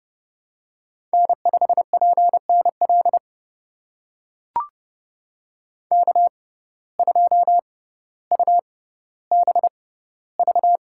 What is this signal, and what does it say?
Morse code, heard as a keyed tone, about 30 wpm, spaced as for 15 wpm, 700 Hz. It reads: N5PNL K2UBV
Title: USA Call Signs - 5 Characters - 6x Character Spacing - 30wpm